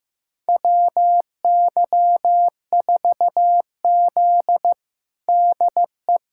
Morse code, heard as a keyed tone, about 15 wpm, 700 Hz. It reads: WY4Z DE
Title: USA Callsign DE Callsign - 4 Characters - 15wpm